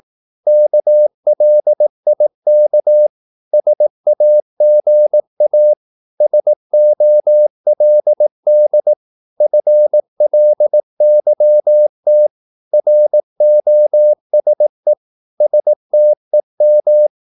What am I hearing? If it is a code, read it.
Morse code, heard as a keyed tone, about 18 wpm, 600 Hz. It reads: KLIK SAGA SOLD FLYT ROSE STEM
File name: Korte ord DK 18wpm